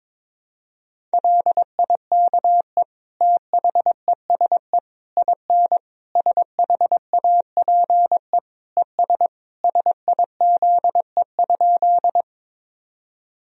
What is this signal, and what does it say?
Morse code, heard as a keyed tone, about 22 wpm, 700 Hz. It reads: LIKE THESE IN SHAPE ES SIZE?
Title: Sentences from Top 500 Words and Abbreviations 22wpm